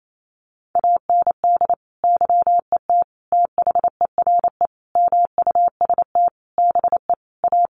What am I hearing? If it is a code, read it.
Morse code, heard as a keyed tone, about 28 wpm, 700 Hz. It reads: AND YET THERE MUST BE A
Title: Sentences from Top 1300 Words 28wpm